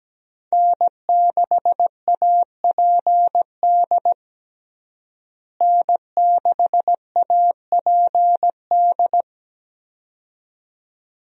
Morse code, heard as a keyed tone, about 17 wpm, 700 Hz. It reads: N6APD N6APD
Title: USA Call Signs - 5 Characters - Repeated 2x - 17wpm